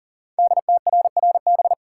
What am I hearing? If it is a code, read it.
Morse code, heard as a keyed tone, about 40 wpm, 690 Hz. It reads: DTRRB
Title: Code Groups - 5 Letters - 40wpm